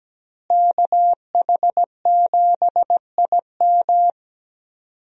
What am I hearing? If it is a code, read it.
Morse code, heard as a keyed tone, about 17 wpm, 700 Hz. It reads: KH7IM
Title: USA Call Signs - 5 Characters - Repeated 3x - 17wpm